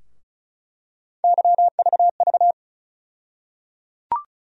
Morse code, heard as a keyed tone, about 34 wpm, 700 Hz. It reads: YVV
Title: USA Call Sign Postfixes - 3 Letters 35wpm